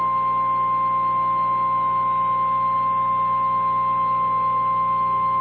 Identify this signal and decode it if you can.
Emergency Alert System (SAME) tones (the 1050 Hz attention tone), with noise. It is heard from 0.7 to 5.1 s.